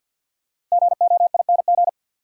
Morse code, heard as a keyed tone, about 46 wpm, 700 Hz. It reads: GOING
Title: Sentences from Top 500 Words - New Word Drill - 50wpm